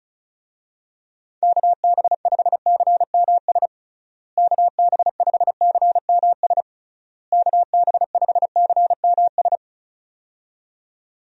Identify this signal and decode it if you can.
Morse code, heard as a keyed tone, about 36 wpm, 700 Hz. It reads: KB5CMS KB5CMS KB5CMS
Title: USA Call Signs - 6 Characters - Repeated 3x - 35wpm